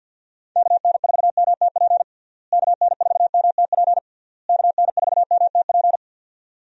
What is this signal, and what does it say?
Morse code, heard as a keyed tone, about 52 wpm, 700 Hz. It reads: KN4MTP KN4MTP KN4MTP
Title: USA Call Signs - 6 Characters - Repeated 3x - 50wpm